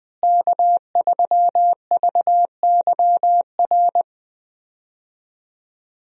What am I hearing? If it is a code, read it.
Morse code, heard as a keyed tone, about 20 wpm, 700 Hz. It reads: K3VYR
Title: USA Call Signs - 5 Characters - Repeated 3x - 20wpm